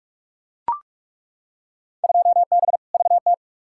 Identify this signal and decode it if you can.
Morse code, heard as a keyed tone, about 42 wpm, 700 Hz. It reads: 2B VT
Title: Field Day Exchanges Encore - 45wpm